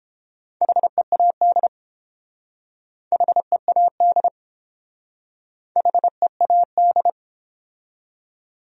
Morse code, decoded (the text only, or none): HEAD HEAD HEAD